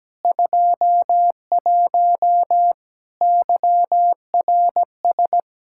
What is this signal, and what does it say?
Morse code, heard as a keyed tone, about 17 wpm, 700 Hz. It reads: 21 YRS
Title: Practice QSO - Exchange 3 - More Info - 17wpm